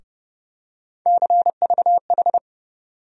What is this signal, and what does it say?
Morse code, heard as a keyed tone, about 30 wpm, 700 Hz. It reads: CVH
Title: USA Call Sign Postfixes - 3 Letters 30wpm